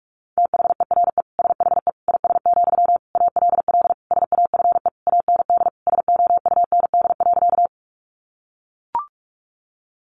Morse code, heard as a keyed tone, about 44 wpm, 700 Hz. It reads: THERE SHE IS, ALL SAFE AND SOUND.
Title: Sentences from Top 900 Words 45wpm